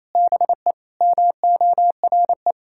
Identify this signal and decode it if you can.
Morse code, heard as a keyed tone, about 28 wpm, 700 Hz. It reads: BE MORE
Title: Sentences from Top 1000 Words - Review Part 6 - 28wpm